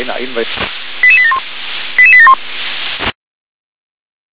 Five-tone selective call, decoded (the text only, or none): 88871 88871